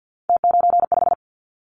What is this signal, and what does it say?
Morse code, heard as a keyed tone, about 50 wpm, 700 Hz. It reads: T95
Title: Callsign Prefixes for Top 50 DXCC - 50wpm